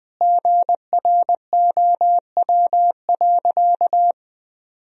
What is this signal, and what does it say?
Morse code, heard as a keyed tone, about 21 wpm, 700 Hz. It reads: GROW.
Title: Sentences from Top 1000 Words - Review Part 1 - 20wpm